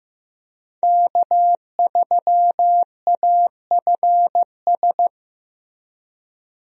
Morse code, heard as a keyed tone, about 15 wpm, 700 Hz. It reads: K3AFS
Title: USA Call Signs - 5 Characters - Repeated 2x - 15wpm